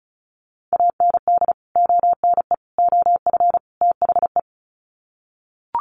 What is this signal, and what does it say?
Morse code, heard as a keyed tone, about 34 wpm, 700 Hz. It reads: AND ONE OF THE
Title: Sets of 4 Words 35wpm